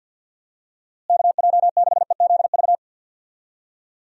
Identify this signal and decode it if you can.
Morse code, heard as a keyed tone, about 52 wpm, 700 Hz. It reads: KJ6EZV
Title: USA Call Signs - 6 Characters - Repeated 2x - 50wpm